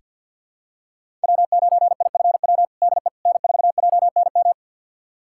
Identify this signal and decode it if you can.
Morse code, heard as a keyed tone, about 48 wpm, 700 Hz. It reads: W9IFW DE N4JNM
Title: USA Callsign DE Callsign - 5 Characters - 50wpm